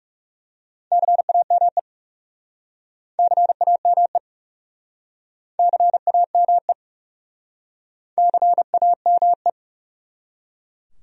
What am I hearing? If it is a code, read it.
Morse code, heard as a keyed tone, about 40 wpm, 700 Hz. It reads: CAME CAME CAME CAME